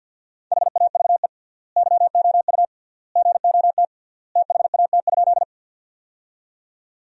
Morse code, heard as a keyed tone, about 50 wpm, 700 Hz. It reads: HAVE YOU GOT THAT?
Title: 3x Word-Spacing - Sentences from Top 300 Words - 50wpm